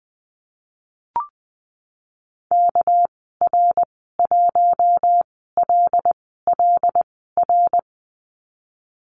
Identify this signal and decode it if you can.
Morse code, heard as a keyed tone, about 20 wpm, spaced as for 15 wpm, 700 Hz. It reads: KR1LLR
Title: 231 Lesson 37g - Letter x - Call Signs 20wpm